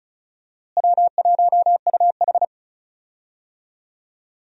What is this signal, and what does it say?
Morse code, heard as a keyed tone, about 36 wpm, 700 Hz. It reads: W1UH
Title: USA Call Signs - 4 Characters - Repeated 2x - 35wpm